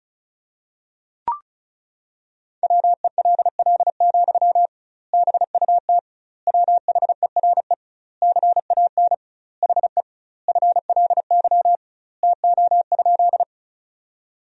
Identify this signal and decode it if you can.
Morse code, heard as a keyed tone, about 36 wpm, 700 Hz. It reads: WELL, BUT WHERE CAN HE FLY TO?
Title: 2x Word-Spacing - Sentences from Top 500 Words - 35wpm